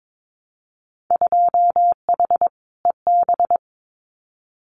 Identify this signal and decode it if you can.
Morse code, heard as a keyed tone, about 22 wpm, 700 Hz. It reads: 2H EB
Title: Winter Field Day Exchanges Encore 22wpm